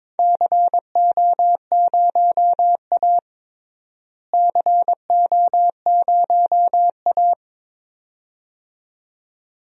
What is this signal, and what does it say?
Morse code, heard as a keyed tone, about 22 wpm, 700 Hz. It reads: CO0A CO0A
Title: DXCC-100 Call Signs - 4 Characters - Repeated 2x - 22wpm